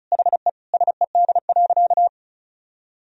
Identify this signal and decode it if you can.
Morse code, heard as a keyed tone, about 36 wpm, 700 Hz. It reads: HE SED.
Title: Sentences from Top 200 Words and Abbreviations 35wpm